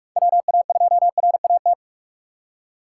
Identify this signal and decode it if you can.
Morse code, heard as a keyed tone, about 46 wpm, 700 Hz. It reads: WA2RAT
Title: USA Call Signs - 6 Characters - 3x Character Spacing - 45wpm